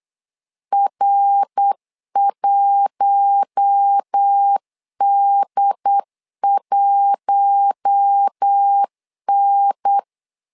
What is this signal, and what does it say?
Morse code, heard as a keyed tone, about 8 wpm, 800 Hz. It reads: R1D1N
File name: LEZIONI / 4 Q1 8wpm 3035